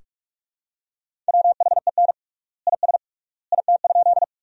Morse code, heard as a keyed tone, about 41 wpm, 700 Hz. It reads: WHEN IS IT?
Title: 3x Word-Spacing - Sentences from Top 100 Words - 45wpm